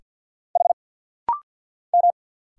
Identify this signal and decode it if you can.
Morse code, heard as a keyed tone, about 50 wpm, 700 Hz. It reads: H M
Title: Single Letter-Number - Rapid-Fire 50wpm